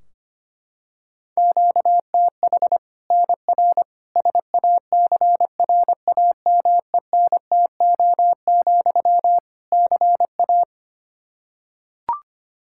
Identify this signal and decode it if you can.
Morse code, heard as a keyed tone, about 25 wpm, 700 Hz. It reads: QTH NR SACRAMENTO, CA
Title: Practice QSO - Exchange 1 - Beginning - 25wpm